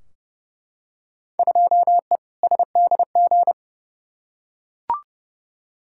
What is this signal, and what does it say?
Morse code, heard as a keyed tone, about 31 wpm, 700 Hz. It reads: 2E SDG
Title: Field Day Exchanges - 30wpm